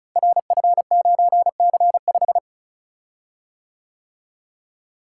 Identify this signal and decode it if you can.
Morse code, heard as a keyed tone, about 36 wpm, 700 Hz. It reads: RF9C5
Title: Code Groups - 5 Letter:Numbers - 35wpm